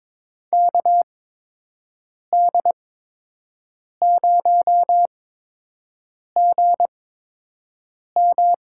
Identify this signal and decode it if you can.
Morse code, heard as a keyed tone, about 22 wpm, spaced as for 6 wpm, 700 Hz. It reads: KD0GM
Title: USA Call Signs - 6 Characters - 8x Character Spacing - 22wpm